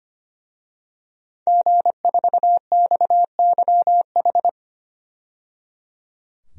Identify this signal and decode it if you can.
Morse code, heard as a keyed tone, about 25 wpm, 700 Hz. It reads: G4XYH
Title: DXCC-100 Call Signs - 5 Characters - 25wpm